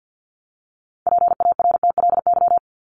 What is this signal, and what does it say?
Morse code, heard as a keyed tone, about 48 wpm, 700 Hz. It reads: PARTLY
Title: LY Suffix - Top 100 Words with Base Words 50wpm